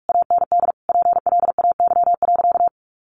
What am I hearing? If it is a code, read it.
Morse code, heard as a keyed tone, about 44 wpm, 700 Hz. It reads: AND PLAY.
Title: Sentences from Top 1300 Words 45wpm